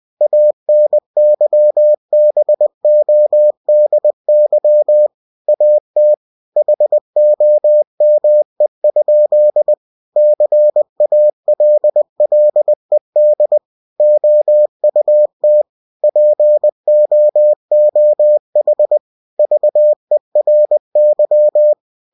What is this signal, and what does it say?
Morse code, heard as a keyed tone, about 20 wpm, 600 Hz. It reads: ANYBODY AT HOME? CALLED OUT POOH VERY
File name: cw-books / winnie-the-pooh / winnie-20wpm-page-0013